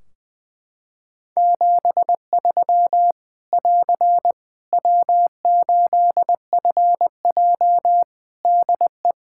Morse code, heard as a keyed tone, about 20 wpm, 700 Hz. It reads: 73 <AR> W8FJ DE
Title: Practice QSO - Exchange 4 - Ending - 20wpm